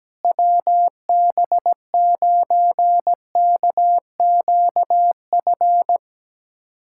Morse code, heard as a keyed tone, about 17 wpm, 700 Hz. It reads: WB9KQF
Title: USA Callsign DE Callsign - 6 Characters - 17wpm